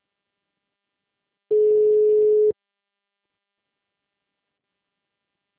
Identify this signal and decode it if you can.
Ringback tone (Europe, 425 Hz).
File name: out- -2007- - - .45647